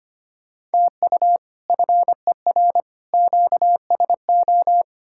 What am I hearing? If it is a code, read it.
Morse code, heard as a keyed tone, about 33 wpm, 700 Hz. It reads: TU FER QSO